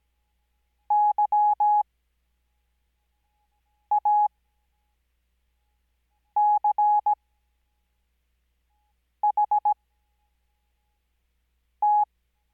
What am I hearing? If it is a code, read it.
Morse code, heard as a keyed tone, about 17 wpm, spaced as for 4 wpm, 840 Hz. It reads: YACHT
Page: Farnsworth Spacing: 15WPM Character speed Spaced at 5WPM.